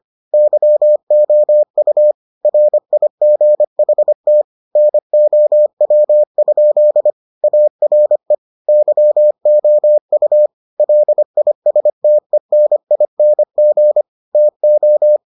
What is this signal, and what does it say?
Morse code, heard as a keyed tone, about 25 wpm, 600 Hz. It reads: YOU RIGHT NOW? ARE YOU LISTENING TO